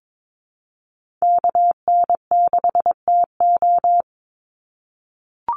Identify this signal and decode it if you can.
Morse code, heard as a keyed tone, about 22 wpm, 700 Hz. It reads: KN6TO
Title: USA Call Signs 22wpm